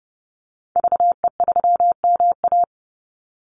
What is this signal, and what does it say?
Morse code, heard as a keyed tone, about 30 wpm, 700 Hz. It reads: VE3MA
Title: Call Signs - 30wpm